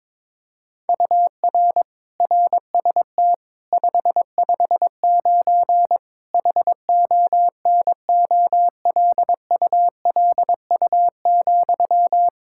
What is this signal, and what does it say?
Morse code, heard as a keyed tone, about 22 wpm, 700 Hz. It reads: UR RST 559 HONOLULU,
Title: Practice QSO - Exchange 1 - Beginning - 22wpm